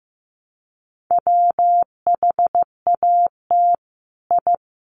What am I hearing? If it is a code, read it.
Morse code, heard as a keyed tone, about 15 wpm, 700 Hz. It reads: WHAT I
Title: Sets of 5 Words Encore2 15wpm